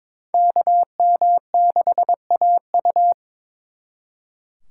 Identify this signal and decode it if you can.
Morse code, heard as a keyed tone, about 26 wpm, 700 Hz. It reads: KM6AU
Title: Speed Racing - USA 2x2 Call Signs 22wpm